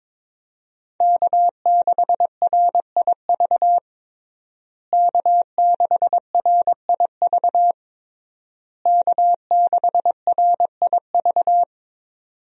Morse code, heard as a keyed tone, about 22 wpm, 700 Hz. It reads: K6RIV K6RIV K6RIV
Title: USA Call Signs - 5 Characters - Repeated 3x - 22wpm